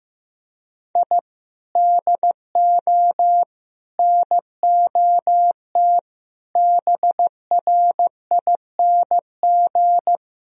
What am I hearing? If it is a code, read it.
Morse code, heard as a keyed tone, about 15 wpm, 700 Hz. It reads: I DO NOT BRING
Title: Sentences from Top 1000 Words - Review Part 3 - 15wpm